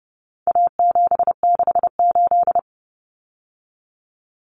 Morse code, heard as a keyed tone, about 30 wpm, 700 Hz. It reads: A768
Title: Common Cut Numbers - 4-Digit 30wpm